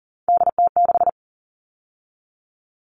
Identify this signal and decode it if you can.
Morse code, heard as a keyed tone, about 40 wpm, 700 Hz. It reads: DT6
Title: All Cut Numbers - 3-Digit 40wpm